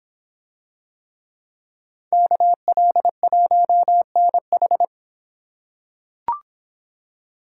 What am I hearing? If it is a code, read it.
Morse code, heard as a keyed tone, about 23 wpm, 700 Hz. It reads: KL1NH